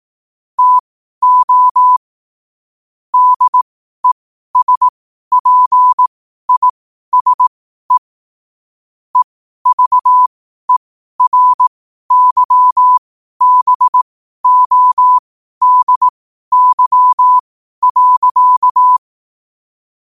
Morse code, heard as a keyed tone, about 18 wpm, spaced as for 12 wpm, 1000 Hz. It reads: TO DESPISE EVERYBODY.
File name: qotdfiles / 12wpm